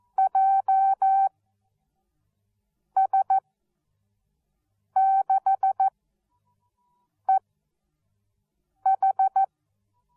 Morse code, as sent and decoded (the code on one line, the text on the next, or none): .--- ... -.... ....
JS6EH